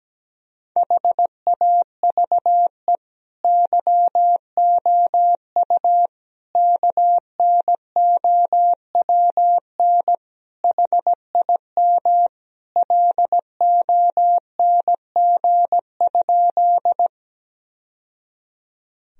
Morse code, heard as a keyed tone, about 17 wpm, 700 Hz. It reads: HAVE YOU KNOWN HIM LONG?
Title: Sentences from Top 1000 Words - Review Part 3 - 17wpm